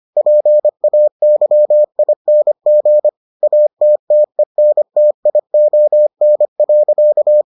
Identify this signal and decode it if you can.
Morse code, heard as a keyed tone, about 25 wpm, 600 Hz. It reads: PAYING ATTENTION.